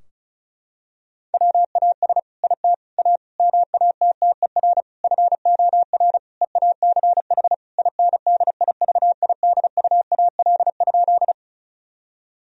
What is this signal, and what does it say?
Morse code, heard as a keyed tone, about 36 wpm, 700 Hz. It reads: WAS IT A MATTER FOR EACH INDIVIDUAL?
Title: Sentences from Top 1000 Words - Review Part 4 - 35wpm